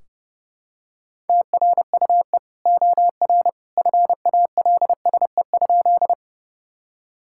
Morse code, heard as a keyed tone, about 32 wpm, 700 Hz. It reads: TRUE OR FALSE?
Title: Sentences from Top 1600 Words 30wpm